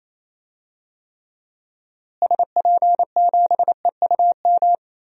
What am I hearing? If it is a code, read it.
Morse code, heard as a keyed tone, about 28 wpm, 700 Hz. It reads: SP7EUM
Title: DXCC-100 Call Signs - 6 Characters - Repeated 2x - 28wpm